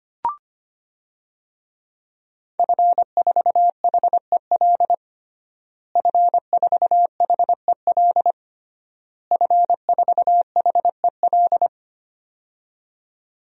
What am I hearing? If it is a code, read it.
Morse code, heard as a keyed tone, about 25 wpm, 700 Hz. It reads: F4HEL F4HEL F4HEL
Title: DXCC-100 Call Signs - 5 Characters - Repeated 3x - 25wpm